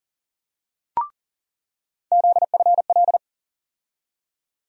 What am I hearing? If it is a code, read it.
Morse code, heard as a keyed tone, about 40 wpm, 700 Hz. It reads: ZFL
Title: USA Call Sign Postfixes - 3 Letters 40wpm